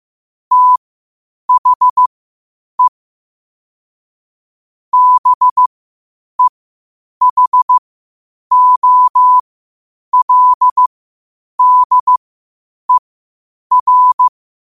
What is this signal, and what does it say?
Morse code, heard as a keyed tone, about 15 wpm, spaced as for 8 wpm, 1000 Hz. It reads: THE BEHOLDER
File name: qotdfiles / 8wpm